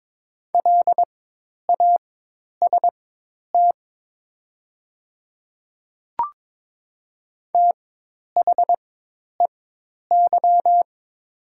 Morse code, heard as a keyed tone, about 22 wpm, spaced as for 10 wpm, 700 Hz. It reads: LAST THEY
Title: 4x Character Spacing - Top 500 Words - 22wpm